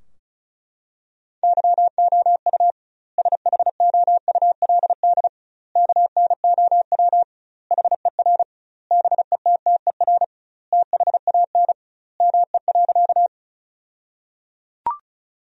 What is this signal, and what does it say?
Morse code, heard as a keyed tone, about 34 wpm, 700 Hz. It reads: YOU SHOULD KNOW HER BETTER THAN ME.
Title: 2x Word-Spacing - Sentences from Top 400 Words - 35wpm